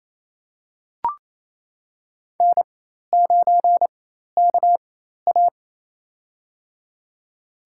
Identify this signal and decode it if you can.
Morse code, heard as a keyed tone, about 28 wpm, spaced as for 13 wpm, 700 Hz. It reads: N9KA
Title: USA Call Signs - 4 Characters - 4x Character Spacing - 28wpm